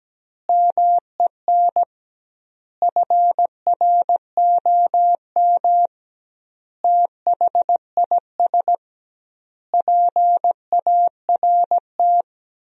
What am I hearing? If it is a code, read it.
Morse code, heard as a keyed tone, about 17 wpm, 700 Hz. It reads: MEN FROM THIS PART
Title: 2x Word-Spacing - Sentences from Top 300 Words - 17wpm